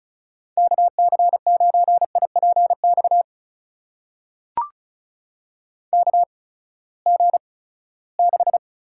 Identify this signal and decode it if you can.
Morse code, heard as a keyed tone, about 36 wpm, spaced as for 20 wpm, 700 Hz. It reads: KC9IPX KG6
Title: USA Call Signs - 6 Characters - 8x Character Spacing - 35wpm